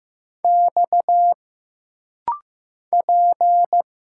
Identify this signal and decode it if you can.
Morse code, heard as a keyed tone, about 15 wpm, 700 Hz. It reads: X P
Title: Single Letters - Rapid-Fire 15wpm